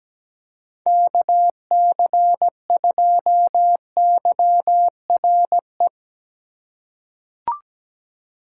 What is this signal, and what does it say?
Morse code, heard as a keyed tone, about 17 wpm, 700 Hz. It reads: KC2YRE
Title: USA Call Signs - 6 Characters - 4x Character Spacing - 17wpm